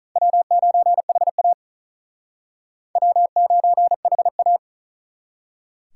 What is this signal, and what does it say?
Morse code, heard as a keyed tone, about 45 wpm, 700 Hz. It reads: W9HA W9HA